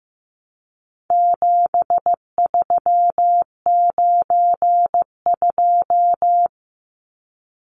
Morse code, heard as a keyed tone, about 15 wpm, 700 Hz. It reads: 7392
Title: Numbers - 4-Digit 15wpm